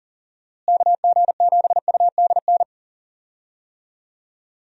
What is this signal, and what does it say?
Morse code, heard as a keyed tone, about 39 wpm, 700 Hz. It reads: KG7UDN
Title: USA Call Signs - 6 Characters - Repeated 3x - 40wpm